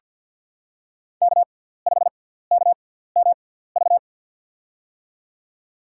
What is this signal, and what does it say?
Morse code, heard as a keyed tone, about 47 wpm, spaced as for 17 wpm, 700 Hz. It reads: K5KMV
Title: USA Call Signs - 5 Characters - 6x Character Spacing - 50wpm